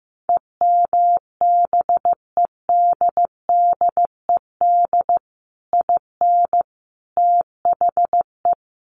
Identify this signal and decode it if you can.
Morse code, heard as a keyed tone, about 15 wpm, 700 Hz. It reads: EMBEDDED IN THE
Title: A Sound Of Thunder 15wpm